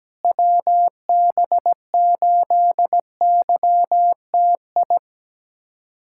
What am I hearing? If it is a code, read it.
Morse code, heard as a keyed tone, about 17 wpm, 700 Hz. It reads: WB8YTI
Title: USA Callsign DE Callsign - 6 Characters - 17wpm